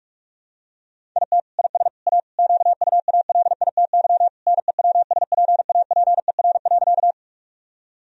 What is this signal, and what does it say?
Morse code, heard as a keyed tone, about 42 wpm, 700 Hz. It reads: IT IS A QUALITY NEWSPAPER.
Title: Sentences from Top 1000 Words - Review Part 4 - 45wpm